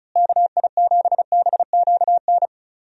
Morse code, heard as a keyed tone, about 36 wpm, 700 Hz. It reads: KI7BQN
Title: USA Call Signs - 6 Characters - Repeated 3x - 35wpm